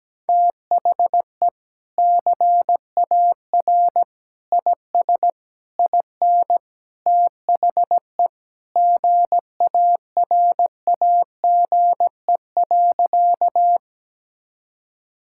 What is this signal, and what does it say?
Morse code, heard as a keyed tone, about 17 wpm, 700 Hz. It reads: THE CAR IS IN THE GARAGE.
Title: Sentences from Top 1600 Words 17wpm